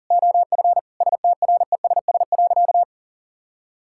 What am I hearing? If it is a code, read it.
Morse code, heard as a keyed tone, about 40 wpm, 700 Hz. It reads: OF STRESS.
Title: Sentences from Top 1000 Words - Review Part 6 - 40wpm